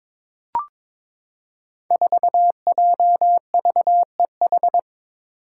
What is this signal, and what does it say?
Morse code, heard as a keyed tone, about 22 wpm, 700 Hz. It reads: 4JVEH
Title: Five Challenging Letter-Number Combinations 22wpm